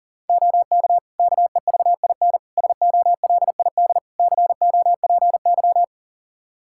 Morse code, heard as a keyed tone, about 40 wpm, 700 Hz. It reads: OK KEVIN SOLID COPY
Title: Practice QSO - Exchange 3 - More Info - 40wpm